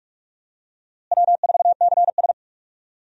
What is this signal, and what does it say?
Morse code, heard as a keyed tone, about 46 wpm, 700 Hz. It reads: W4CS
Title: USA Call Signs - 4 Characters - Repeated 2x - 45wpm